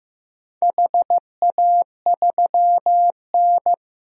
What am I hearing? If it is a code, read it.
Morse code, heard as a keyed tone, about 15 wpm, 700 Hz. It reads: HA3N
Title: DXCC-100 Call Signs - 4 Characters - Repeated 3x - 15wpm